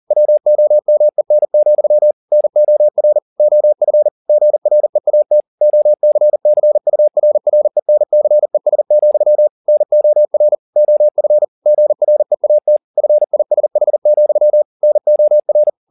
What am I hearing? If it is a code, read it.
Morse code, heard as a keyed tone, about 40 wpm, 600 Hz. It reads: WOMEN, NOR OF GREAT OCCURRENCES, NOR OF GREAT FISH, NOR